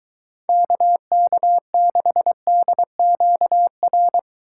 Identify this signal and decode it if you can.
Morse code, heard as a keyed tone, about 23 wpm, 700 Hz. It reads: KK6DQR